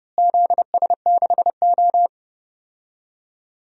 Morse code, heard as a keyed tone, about 32 wpm, 700 Hz. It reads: ZS6O
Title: DXCC-100 Call Signs - 4 Characters - Repeated 3x - 30wpm